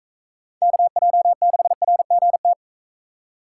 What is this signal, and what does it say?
Morse code, heard as a keyed tone, about 42 wpm, 700 Hz. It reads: KJ6RGT